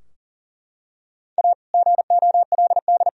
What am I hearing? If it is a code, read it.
Morse code, heard as a keyed tone, about 40 wpm, 700 Hz. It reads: A GOLD
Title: Sentences from Top 1000 Words - Review Part 3 - 40wpm